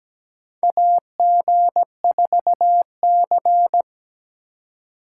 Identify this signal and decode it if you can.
Morse code, heard as a keyed tone, about 17 wpm, 700 Hz. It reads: AG4C
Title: USA Call Signs - 2x Character Spacing - 17wpm